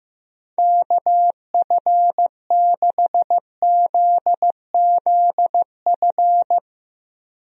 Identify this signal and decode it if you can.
Morse code, heard as a keyed tone, about 15 wpm, 700 Hz. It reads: KF6ZZF
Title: USA Call Signs - 6 Characters - Repeated 3x - 15wpm